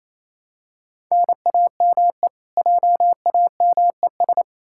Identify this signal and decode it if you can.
Morse code, heard as a keyed tone, about 28 wpm, 700 Hz. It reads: NAME JAMES
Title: Practice QSO - Exchange 1 - Beginning - 28wpm